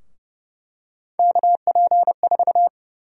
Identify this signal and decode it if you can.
Morse code, heard as a keyed tone, about 30 wpm, 700 Hz. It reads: KP4
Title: USA Call Sign Prefixes - 30wpm